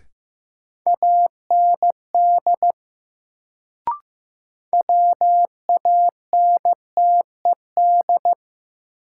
Morse code, heard as a keyed tone, about 15 wpm, 700 Hz. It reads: AND WANTED
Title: Sets of 2 Words - New Word Drill - 15wpm